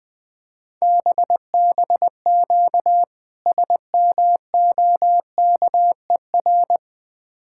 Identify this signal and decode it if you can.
Morse code, heard as a keyed tone, about 20 wpm, 700 Hz. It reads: BBQ SMOKER
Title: Difficult Words - Speed-Racing with Hints - 20wpm